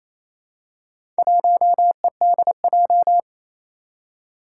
Morse code, heard as a keyed tone, about 27 wpm, 700 Hz. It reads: 1EDJ
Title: Four Challenging Letter-Number Combinations 28wpm